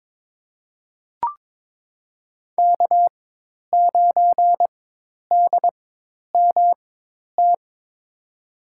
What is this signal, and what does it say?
Morse code, heard as a keyed tone, about 22 wpm, spaced as for 10 wpm, 700 Hz. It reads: K9DMT
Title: USA Call Signs - 5 Characters - 4x Character Spacing - 22wpm